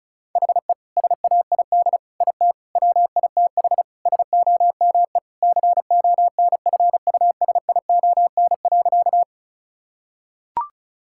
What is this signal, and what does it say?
Morse code, heard as a keyed tone, about 34 wpm, 700 Hz. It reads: HE SAID IT WITH SOME CONFUSION.
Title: Sentences from Top 1600 Words 35wpm